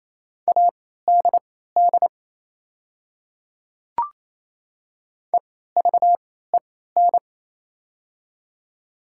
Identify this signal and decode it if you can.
Morse code, heard as a keyed tone, about 28 wpm, spaced as for 16 wpm, 700 Hz. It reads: ADD EVEN
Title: 3x Character Spacing - Top 300 Words - 28wpm